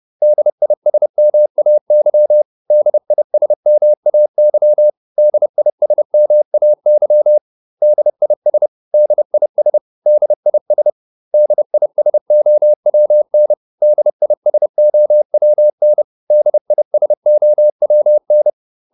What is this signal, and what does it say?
Morse code, heard as a keyed tone, about 30 wpm, 600 Hz. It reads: DISMAY DISMAY DISMAY DIS DIS DIS DISOWN DISOWN DISOWN